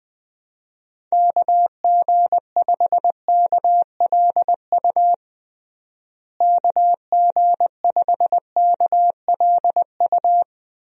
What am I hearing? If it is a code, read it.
Morse code, heard as a keyed tone, about 20 wpm, 700 Hz. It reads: KG5KLU KG5KLU
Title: USA Call Signs - 6 Characters - Repeated 2x - 20wpm